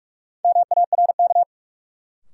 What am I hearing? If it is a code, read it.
Morse code, heard as a keyed tone, about 46 wpm, 700 Hz. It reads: MARK
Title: Sentences from Top 300 Words - New Word Drill - 45wpm